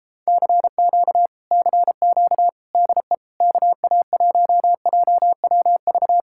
Morse code, heard as a keyed tone, about 33 wpm, 700 Hz. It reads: CQ CQ DE KA1JWV